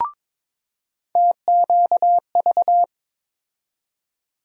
Morse code, heard as a keyed tone, about 22 wpm, 700 Hz. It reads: TQV
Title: USA Call Sign Postfixes - 3 Letters 22wpm